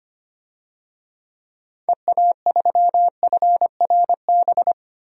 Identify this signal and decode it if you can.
Morse code, heard as a keyed tone, about 25 wpm, 700 Hz. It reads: EA3FRB
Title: DXCC-100 Call Signs - 6 Characters - 25wpm